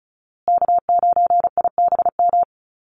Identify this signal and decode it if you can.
Morse code, heard as a keyed tone, about 34 wpm, 700 Hz. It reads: K9IBM
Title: Call Signs Repeated 2x - 35wpm